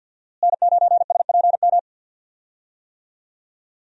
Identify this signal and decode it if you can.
Morse code, heard as a keyed tone, about 51 wpm, 700 Hz. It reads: N9SPM
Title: USA Call Signs - 5 Characters - Repeated 2x - 50wpm